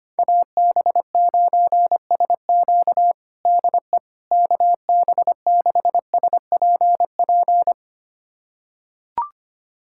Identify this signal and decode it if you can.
Morse code, heard as a keyed tone, about 25 wpm, 700 Hz. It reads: AB9SQ DE KB6SPP
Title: USA Callsign DE Callsign 25wpm